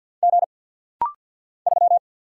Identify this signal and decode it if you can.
Morse code, heard as a keyed tone, about 48 wpm, 700 Hz. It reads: G 3
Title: Single Letter-Number - Rapid-Fire 50wpm